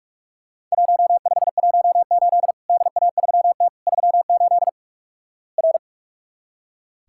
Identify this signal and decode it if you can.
Morse code, heard as a keyed tone, about 46 wpm, 700 Hz. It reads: 1518 DA3T 38 R
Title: Worked All Europe DX Contest - QTC Exchange - 45wpm